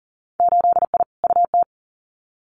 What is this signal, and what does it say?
Morse code, heard as a keyed tone, about 40 wpm, 700 Hz. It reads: 8I UT
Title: Winter Field Day Exchanges Encore 40wpm